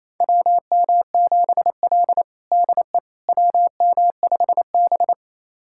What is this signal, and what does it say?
Morse code, heard as a keyed tone, about 28 wpm, 700 Hz. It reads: WM7L DE WM5B
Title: USA Callsign DE Callsign - 4 Characters - 28wpm